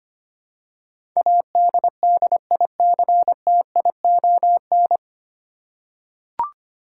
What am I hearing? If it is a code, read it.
Morse code, heard as a keyed tone, about 25 wpm, 700 Hz. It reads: ADDICTION
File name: Nine Letter Words - 25wpm